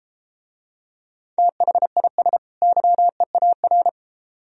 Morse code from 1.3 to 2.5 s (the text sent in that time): THIS